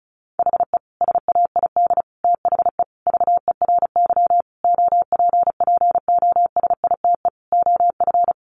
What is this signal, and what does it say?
Morse code, heard as a keyed tone, about 34 wpm, 700 Hz. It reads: HE SAID THE VERY OPPOSITE OF
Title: Sentences from Top 1300 Words 35wpm